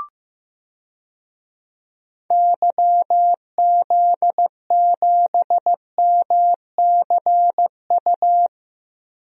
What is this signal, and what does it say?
Morse code, heard as a keyed tone, about 15 wpm, 700 Hz. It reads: YZ7MCU
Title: DXCC-100 Call Signs - 6 Characters - 15wpm